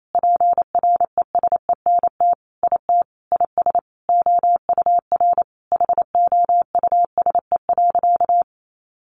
Morse code, heard as a keyed tone, about 28 wpm, 700 Hz. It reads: PRESENT IT IS OUR HOUSE.
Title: Sentences from Top 800 Words 28wpm